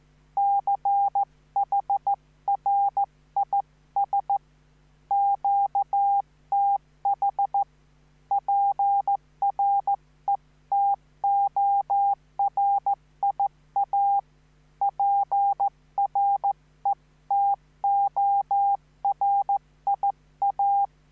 Morse code, heard as a keyed tone, about 15 wpm, 800 Hz. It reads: CHRIS QTH PRETORIA PRETORIA
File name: QSO03